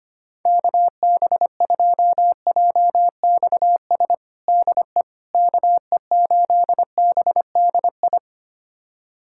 Morse code, heard as a keyed tone, about 25 wpm, 700 Hz. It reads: KB2JXS DE KE8BDI
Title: USA Callsign DE Callsign 25wpm